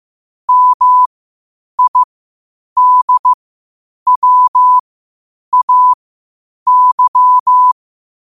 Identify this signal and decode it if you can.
Morse code, heard as a keyed tone, about 15 wpm, spaced as for 8 wpm, 1000 Hz. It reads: MIDWAY
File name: qotdfiles / 8wpm